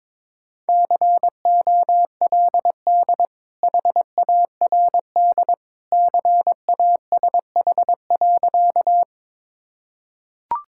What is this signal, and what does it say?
Morse code, heard as a keyed tone, about 22 wpm, 700 Hz. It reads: COLD HARD CASH.
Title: Sentences from Top 1000 Words - Review Part 5 - 22wpm